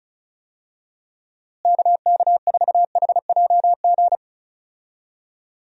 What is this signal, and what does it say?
Morse code, heard as a keyed tone, about 36 wpm, 700 Hz. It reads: KK4HJG
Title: USA Call Signs - 6 Characters - Repeated 3x - 35wpm